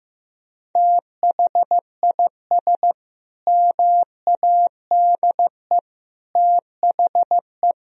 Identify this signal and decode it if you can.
Morse code, heard as a keyed tone, about 16 wpm, 700 Hz. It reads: THIS MADE THE
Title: Sentences from Top 1000 Words - Review Part 2 - 15wpm